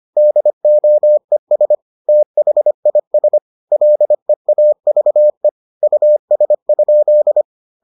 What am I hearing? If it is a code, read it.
Morse code, heard as a keyed tone, about 25 wpm, 600 Hz. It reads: DOES THIS LEAVE US?